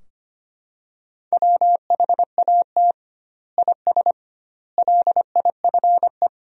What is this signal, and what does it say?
Morse code, heard as a keyed tone, about 25 wpm, 700 Hz. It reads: WHAT IS LIFE
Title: 2x Word-Spacing - Sentences from Top 300 Words - 25wpm